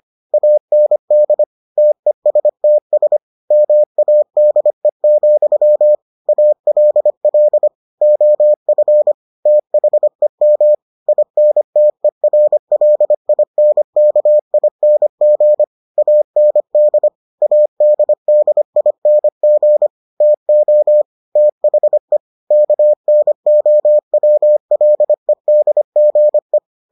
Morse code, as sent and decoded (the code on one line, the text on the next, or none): .- -. -.. / - . ... - ... / -- .- -.. --..-- / .- .-.. .-.. / --- ..-. / - .... -- / .. -. - . .-. .-.. .. -. -.- .. -. --. / .- -. -.. / .- -.. -.. .. -. --. / - --- / - .... / -.- -. --- .-- .-.. -.. --. .
AND TESTS MADE, ALL OF THEM INTERLINKING AND ADDING TO THE KNOWLEDGE